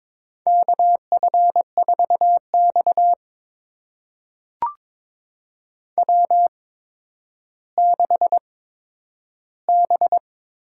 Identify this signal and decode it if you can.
Morse code, heard as a keyed tone, about 22 wpm, spaced as for 11 wpm, 700 Hz. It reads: KF4X W6B
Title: USA Call Signs - 4 Characters - 8x Character Spacing - 22wpm